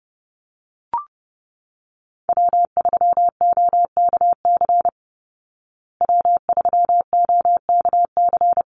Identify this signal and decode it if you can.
Morse code, heard as a keyed tone, about 30 wpm, 700 Hz. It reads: W3OKC W3OKC
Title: Call Signs Repeated 3x - 30wpm